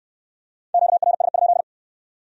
Morse code, heard as a keyed tone, about 64 wpm, 700 Hz. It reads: QRS?